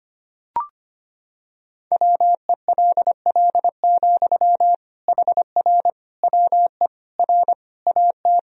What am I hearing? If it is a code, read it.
Morse code, heard as a keyed tone, about 25 wpm, 700 Hz. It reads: WELL, HR WE R AT
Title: Sentences from Top 300 Words and Abbreviations 25wpm